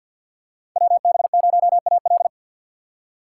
Morse code, heard as a keyed tone, about 49 wpm, 700 Hz. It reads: WB0AL
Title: USA Call Signs - 5 Characters - 4x Character Spacing - 50wpm